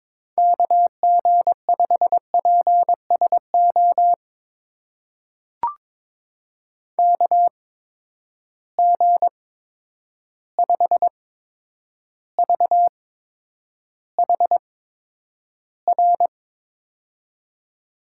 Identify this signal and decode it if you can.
Morse code, heard as a keyed tone, about 22 wpm, spaced as for 9 wpm, 700 Hz. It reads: KG5PSO KG5VHR
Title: USA Call Signs - 6 Characters - 8x Character Spacing - 22wpm